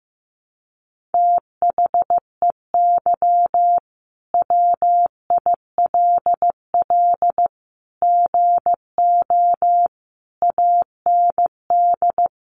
Morse code, heard as a keyed tone, about 15 wpm, 700 Hz. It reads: THEY WILL GO AND
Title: Sentences from Top 1100 Words 15wpm